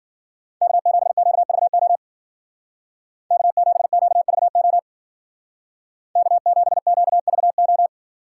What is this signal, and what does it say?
Morse code, heard as a keyed tone, about 50 wpm, 700 Hz. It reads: K7QVO K7QVO K7QVO